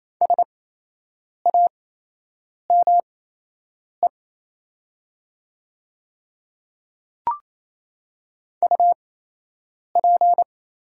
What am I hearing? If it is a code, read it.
Morse code, heard as a keyed tone, about 28 wpm, spaced as for 8 wpm, 700 Hz. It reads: SAME UP
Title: 8x Character Spacing - Top 200 Words - 28wpm